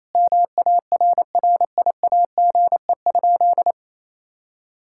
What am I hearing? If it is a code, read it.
Morse code, heard as a keyed tone, about 28 wpm, 700 Hz. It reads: MARRIAGE?
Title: Sentences from Top 1000 Words - Review Part 5 - 28wpm